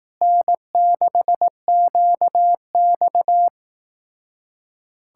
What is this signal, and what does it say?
Morse code, heard as a keyed tone, about 18 wpm, 700 Hz. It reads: N6QX
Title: Speed-Racing Common QSO Phrases Encore - 15wpm